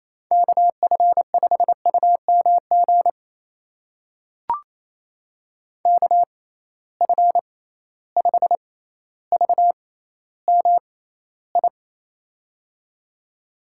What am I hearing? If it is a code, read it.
Morse code, heard as a keyed tone, about 28 wpm, spaced as for 14 wpm, 700 Hz. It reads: KF5UMG KF5VMI
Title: USA Call Signs - 6 Characters - 6x Character Spacing - 28wpm